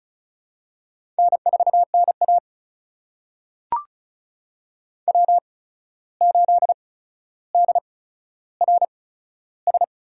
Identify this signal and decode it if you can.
Morse code, heard as a keyed tone, about 36 wpm, spaced as for 14 wpm, 700 Hz. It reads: N4NA W8DRS
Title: USA Call Signs - 8x Character Spacing - 35wpm